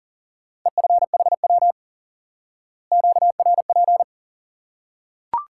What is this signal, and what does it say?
Morse code, heard as a keyed tone, about 37 wpm, 700 Hz. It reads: EFHW QRP
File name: IQE Course 3 - Lesson 89 - YAGI - Two Elements - 40wpm